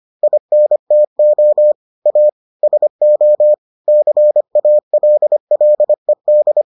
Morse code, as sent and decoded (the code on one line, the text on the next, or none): .. -. - --- / .- / ... --- / -.-. .- .-.. .-.. -..
INTO A SO CALLED